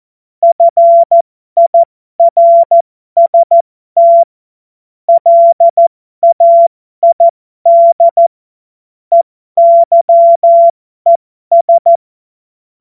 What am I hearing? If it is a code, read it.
Morse code, heard as a keyed tone, about 14 wpm, spaced as for 12 wpm, 680 Hz. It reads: FIRST LAID EYES
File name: Morsekurs / Witze